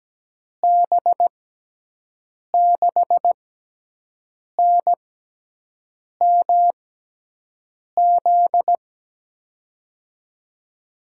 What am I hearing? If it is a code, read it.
Morse code, heard as a keyed tone, about 17 wpm, spaced as for 6 wpm, 700 Hz. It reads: B6NMZ
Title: USA Call Signs - 6 Characters - 6x Character Spacing - 17wpm